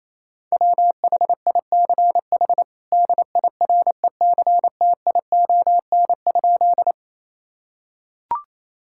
Morse code, heard as a keyed tone, about 28 wpm, 700 Hz. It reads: WHICH DIRECTION?
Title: Sentences from Top 1000 Words - Review Part 5 - 28wpm